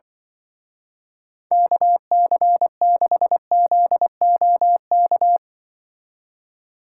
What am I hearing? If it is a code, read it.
Morse code, heard as a keyed tone, about 25 wpm, 700 Hz. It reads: KC6ZOK
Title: Speed Racing - USA 2x3 Call Signs 20wpm